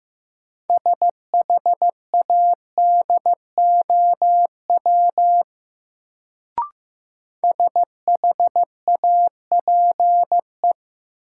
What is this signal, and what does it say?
Morse code, heard as a keyed tone, about 15 wpm, 700 Hz. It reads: SHADOW SHAPE
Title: Sets of 5 Words Encore - New Word Drill - 15wpm